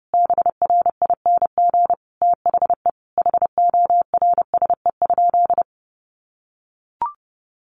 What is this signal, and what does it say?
Morse code, heard as a keyed tone, about 30 wpm, 700 Hz. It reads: BRING THE HORSE?
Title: Sentences from Top 500 Words 30wpm